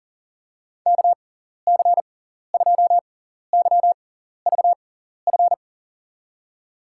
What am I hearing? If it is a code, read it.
Morse code, heard as a keyed tone, about 40 wpm, spaced as for 14 wpm, 700 Hz. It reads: KC2YVF
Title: USA Call Signs - 6x Character Spacing - 40wpm